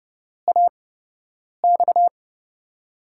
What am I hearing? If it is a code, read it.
Morse code, heard as a keyed tone, about 30 wpm, spaced as for 8 wpm, 700 Hz. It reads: AX
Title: USA Call Signs - 6 Characters - 8x Character Spacing - 30wpm